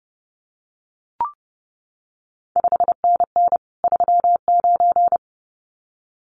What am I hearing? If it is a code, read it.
Morse code, heard as a keyed tone, about 30 wpm, 700 Hz. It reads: 5NN 39
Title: CQ WW DX Contest Exchange - 30wpm